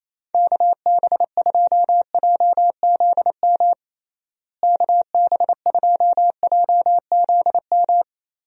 Morse code, heard as a keyed tone, about 28 wpm, 700 Hz. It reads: KB2JZM KB2JZM
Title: USA Call Signs - 6 Characters - Repeated 3x - 28wpm